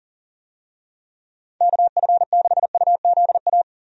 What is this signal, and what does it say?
Morse code, heard as a keyed tone, about 40 wpm, 700 Hz. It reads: KF6UZA
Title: USA Call Signs - 6 Characters - Repeated 3x - 40wpm